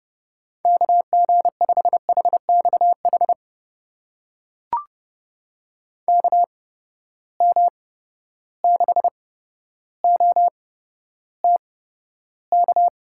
Morse code, heard as a keyed tone, about 30 wpm, spaced as for 13 wpm, 700 Hz. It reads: KG5HXH KM6OTK
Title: USA Call Signs - 6 Characters - 8x Character Spacing - 30wpm